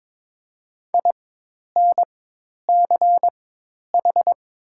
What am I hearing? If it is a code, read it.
Morse code, heard as a keyed tone, about 22 wpm, spaced as for 10 wpm, 700 Hz. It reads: INCH
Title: 4x Character Spacing - Top 500 Words - 22wpm